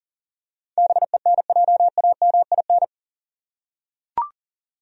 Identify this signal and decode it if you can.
Morse code, heard as a keyed tone, about 40 wpm, 700 Hz. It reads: BENJAMIN
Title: USA Common Names - 40wpm